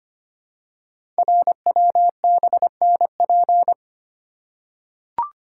Morse code, heard as a keyed tone, about 25 wpm, 700 Hz. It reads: RWBNP
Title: Code Groups - 5 Letters - 25wpm